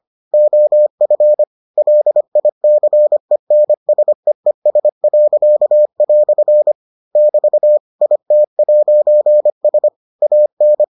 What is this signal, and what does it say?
Morse code, heard as a keyed tone, about 25 wpm, 600 Hz. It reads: OF LICENSEES." <BT> IT'S AN